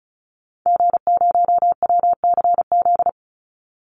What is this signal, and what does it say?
Morse code, heard as a keyed tone, about 34 wpm, 700 Hz. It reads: G0WCZ
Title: Call Signs - 35wpm